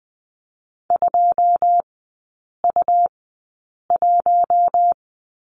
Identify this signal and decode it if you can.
Morse code, heard as a keyed tone, about 19 wpm, spaced as for 8 wpm, 700 Hz. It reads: 2U1
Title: 100 Lesson 17e - Letter u - Three Characters 20wpm